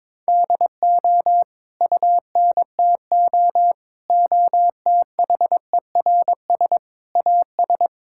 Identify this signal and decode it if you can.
Morse code, heard as a keyed tone, about 22 wpm, 700 Hz. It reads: DO UNTO OTHERS AS
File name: Sets of 9 Words - 22wpm